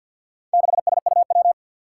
Morse code, heard as a keyed tone, about 53 wpm, 700 Hz. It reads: 6SUW
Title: Four Challenging Letter-Number Combinations 50wpm